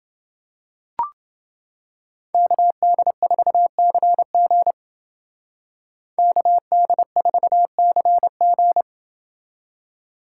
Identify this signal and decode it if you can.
Morse code, heard as a keyed tone, about 29 wpm, 700 Hz. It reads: KD4CG KD4CG